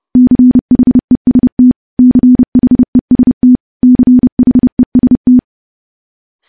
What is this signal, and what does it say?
Morse code, heard as a keyed tone, about 30 wpm, 250 Hz. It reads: CHEST CHEST CHEST
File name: morse / morse-training-30WPM